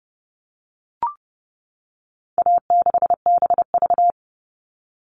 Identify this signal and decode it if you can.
Morse code, heard as a keyed tone, about 30 wpm, 700 Hz. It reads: A6BV
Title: All Cut Numbers - 4-Digit 30wpm